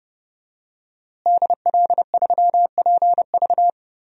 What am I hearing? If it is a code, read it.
Morse code, heard as a keyed tone, about 32 wpm, 700 Hz. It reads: DL3PV
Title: DXCC-100 Call Signs - 5 Characters - Repeated 2x - 30wpm